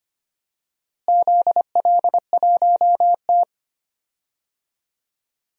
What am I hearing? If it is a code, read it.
Morse code, heard as a keyed tone, about 25 wpm, 700 Hz. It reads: ZL1T
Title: DXCC-100 Call Signs - 4 Characters - Repeated 2x - 25wpm